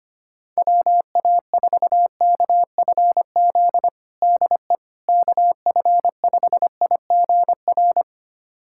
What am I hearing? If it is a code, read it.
Morse code, heard as a keyed tone, about 25 wpm, 700 Hz. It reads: WA4KFZ DE KF5IGR
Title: USA Callsign DE Callsign - 6 Characters - 25wpm